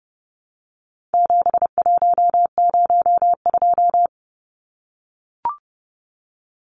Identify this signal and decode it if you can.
Morse code, heard as a keyed tone, about 30 wpm, 700 Hz. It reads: 7102
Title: Numbers - 4-Digit 30wpm